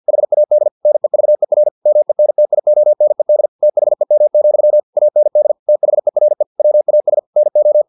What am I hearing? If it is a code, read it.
Morse code, heard as a keyed tone, about 50 wpm, 600 Hz. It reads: HAD NEVER MENTIONED THEM, AND THERE WAS NO
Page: Never 49wpm